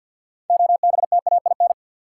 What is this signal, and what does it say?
Morse code, heard as a keyed tone, about 53 wpm, 700 Hz. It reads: OBTAIN
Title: Sentences from Top 1300 Words - New Word Drill Alt - 50wpm